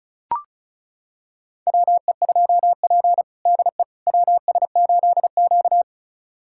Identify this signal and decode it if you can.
Morse code, heard as a keyed tone, about 34 wpm, 700 Hz. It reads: WE2P DE WS8Q
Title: USA Callsign DE Callsign - 4 Characters - 35wpm